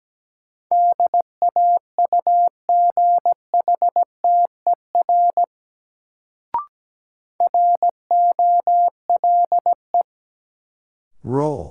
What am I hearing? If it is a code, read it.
Morse code, heard as a keyed tone, about 17 wpm, 700 Hz. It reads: DAUGHTER ROLE
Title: Sentences from Top 1500 Words - New Word Drill Alt - 17wpm